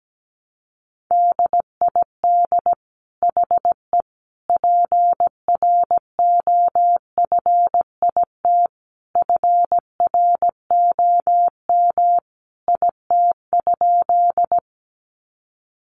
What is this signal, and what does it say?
Morse code, heard as a keyed tone, about 17 wpm, 700 Hz. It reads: DID HE PROFIT FROM IT?
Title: Sentences from Top 800 Words 17wpm